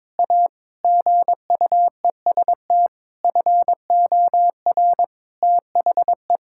Morse code, heard as a keyed tone, about 22 wpm, 700 Hz. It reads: A GUEST FOR THE
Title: Sentences from Top 1600 Words 22wpm